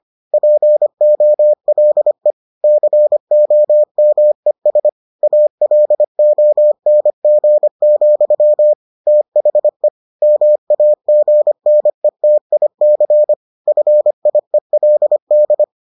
Morse code, heard as a keyed tone, about 25 wpm, 600 Hz. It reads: POLE COMES ALONG, THE MAGNETIC FIELD